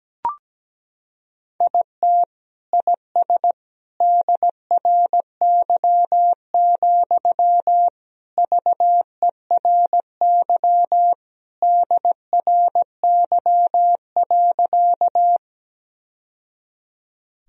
Morse code, heard as a keyed tone, about 17 wpm, 700 Hz. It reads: IT IS DRY, VERY DRY.
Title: Sentences from Top 1000 Words - Review Part 3 - 17wpm